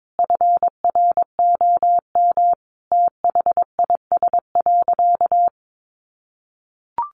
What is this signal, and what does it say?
Morse code, heard as a keyed tone, about 22 wpm, 700 Hz. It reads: FROM THIS.
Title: Sentences from Top 700 Words 22wpm